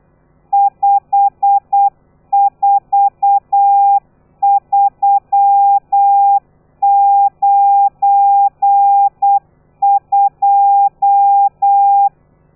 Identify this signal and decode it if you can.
Morse code, heard as a keyed tone, about 8 wpm, 800 Hz. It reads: 54392
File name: telegrafia / i2tfj / Lez12 35